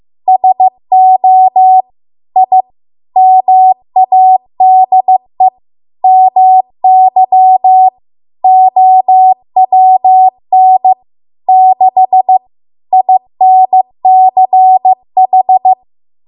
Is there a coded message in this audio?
Yes: Morse code from 0.3 to 15.7 s, heard as a keyed tone, about 15 wpm, 750 Hz. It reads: SO I MADE MY OWN 6 INCH